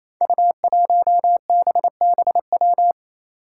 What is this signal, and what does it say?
Morse code, heard as a keyed tone, about 28 wpm, 700 Hz. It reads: U1BBW
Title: Five Challenging Letter-Number Combinations 28wpm